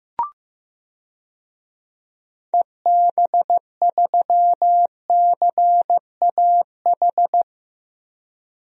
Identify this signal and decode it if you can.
Morse code, heard as a keyed tone, about 15 wpm, 700 Hz. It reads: EB3CAH
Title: DXCC-100 Call Signs - 6 Characters - Repeated 2x - 15wpm